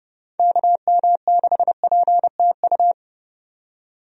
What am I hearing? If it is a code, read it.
Morse code, heard as a keyed tone, about 30 wpm, 700 Hz. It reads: KM6PTU
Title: USA Call Signs - 6 Characters - 30wpm